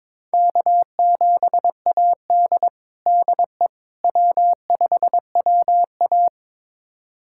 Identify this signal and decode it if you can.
Morse code, heard as a keyed tone, about 22 wpm, 700 Hz. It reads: K7AD DE W5WA
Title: USA Callsign DE Callsign - 4 Characters - 22wpm